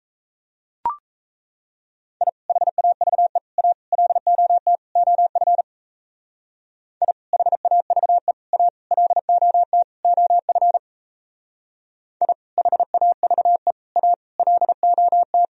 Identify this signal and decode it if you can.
Morse code, heard as a keyed tone, about 34 wpm, 700 Hz. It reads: I HAVE A LOT OF I HAVE A LOT OF I HAVE A LOT
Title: Speed-Racing - Sets of 5 Words 28wpm